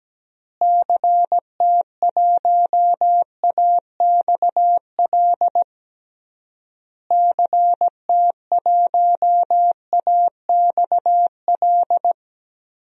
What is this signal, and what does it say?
Morse code, heard as a keyed tone, about 17 wpm, 700 Hz. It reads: CT1AXL CT1AXL
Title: DXCC-100 Call Signs - 6 Characters - Repeated 3x - 17wpm